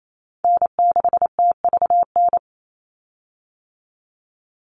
Morse code, heard as a keyed tone, about 28 wpm, 700 Hz. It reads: N6TVN
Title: Call Signs - 28wpm